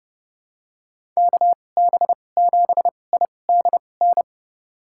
Morse code, heard as a keyed tone, about 31 wpm, spaced as for 22 wpm, 700 Hz. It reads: KB7IDN
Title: USA Call Signs - 6 Characters - 2x Character Spacing - 30wpm